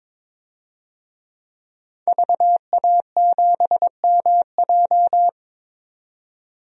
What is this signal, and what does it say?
Morse code, heard as a keyed tone, about 21 wpm, 700 Hz. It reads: VA7MJ
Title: DXCC-100 Call Signs - 5 Characters - 22wpm